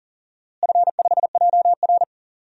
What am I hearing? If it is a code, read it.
Morse code, heard as a keyed tone, about 37 wpm, 700 Hz. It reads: F5JR
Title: DXCC-100 Call Signs - 4 Characters - Repeated 3x - 40wpm